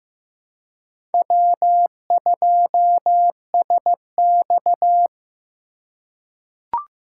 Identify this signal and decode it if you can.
Morse code, heard as a keyed tone, about 15 wpm, 700 Hz. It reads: W2SX
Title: USA Call Signs - 4 Characters - 2x Character Spacing - 15wpm